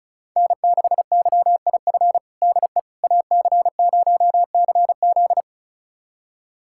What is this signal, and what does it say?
Morse code, heard as a keyed tone, about 34 wpm, 700 Hz. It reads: N6YIF DE AC0CZ
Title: USA Callsign DE Callsign - 5 Characters - 35wpm